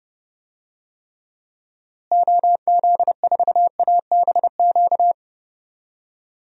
Morse code, heard as a keyed tone, about 30 wpm, 700 Hz. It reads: OZ4ABQ
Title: DXCC-100 Call Signs - 6 Characters - Repeated 2x - 30wpm